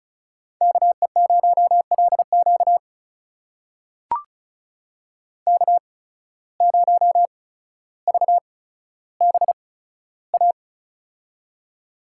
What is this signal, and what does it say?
Morse code, heard as a keyed tone, about 34 wpm, spaced as for 15 wpm, 700 Hz. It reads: KE0LQ K0VBA
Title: USA Call Signs - 5 Characters - 8x Character Spacing - 35wpm